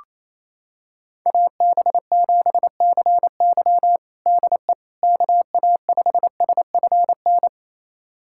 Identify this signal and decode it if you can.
Morse code, heard as a keyed tone, about 28 wpm, 700 Hz. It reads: AB7CY DE KA5SFN
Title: USA Callsign DE Callsign 28wpm